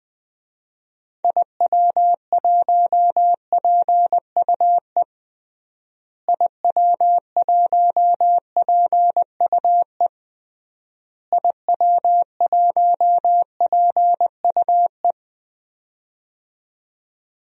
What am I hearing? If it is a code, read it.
Morse code, heard as a keyed tone, about 21 wpm, 700 Hz. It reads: IW1PUE IW1PUE IW1PUE
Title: DXCC-100 Call Signs - 6 Characters - Repeated 3x - 20wpm